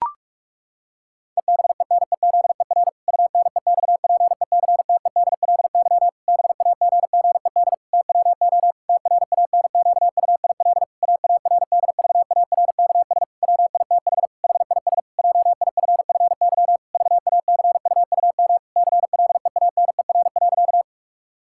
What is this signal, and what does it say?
Morse code, heard as a keyed tone, about 44 wpm, 700 Hz. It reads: EBENEZER UNEXPECTEDLY BAGGED TWO TRANQUIL AARDVARKS WITH HIS JIFFY VACUUM CLEANER.
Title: Pangram Challenge v2 - 45wpm